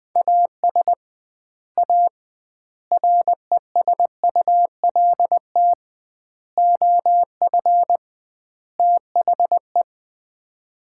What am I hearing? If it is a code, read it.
Morse code, heard as a keyed tone, about 20 wpm, 700 Hz. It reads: AS A RESULT OF THE
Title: 2x Word-Spacing - Sets of 5 Words - 20wpm